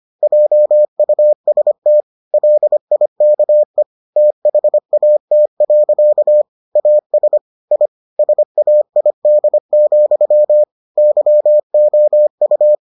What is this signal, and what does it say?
Morse code, heard as a keyed tone, about 25 wpm, 600 Hz. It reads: JUST LIKE THAT. AS I SAID, YOU